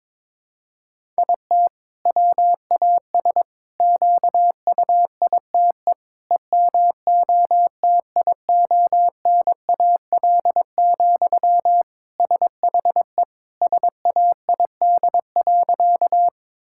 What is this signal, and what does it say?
Morse code, heard as a keyed tone, about 22 wpm, 700 Hz. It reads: IT WAS QUITE EMOTIONAL, SHE SAID.
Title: Sentences from Top 1000 Words - Review Part 5 - 22wpm